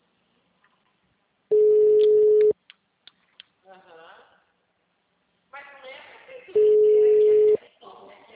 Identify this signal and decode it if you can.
Ringback tone (Europe, 425 Hz), with speech.